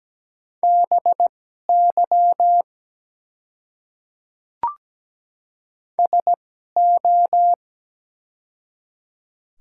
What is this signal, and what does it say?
Morse code, heard as a keyed tone, about 17 wpm, spaced as for 12 wpm, 700 Hz. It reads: BY SO
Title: 2x Character Spacing - Top 100 Words - 17wpm